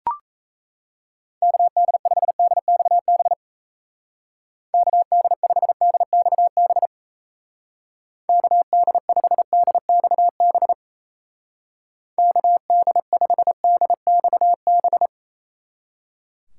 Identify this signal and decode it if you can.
Morse code, heard as a keyed tone, about 35 wpm, 700 Hz. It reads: KD5DXB KD5DXB KD5DXB KD5DXB
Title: Speed Racing - USA 2x3 Call Signs 28wpm